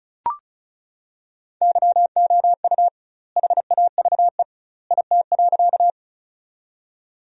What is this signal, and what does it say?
Morse code, heard as a keyed tone, about 34 wpm, 700 Hz. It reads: YOU HAVE IT.
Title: 2x Word-Spacing - Sentences from Top 100 Words - 35wpm